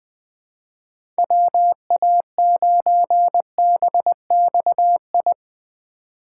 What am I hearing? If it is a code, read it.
Morse code, heard as a keyed tone, about 20 wpm, 700 Hz. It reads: WA9BXI
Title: USA Call Signs - 6 Characters - 6x Character Spacing - 20wpm